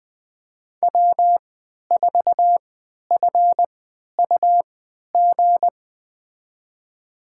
Morse code, heard as a keyed tone, about 20 wpm, spaced as for 11 wpm, 700 Hz. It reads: W4FUG
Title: USA Call Signs - 5 Characters - 3x Character Spacing - 20wpm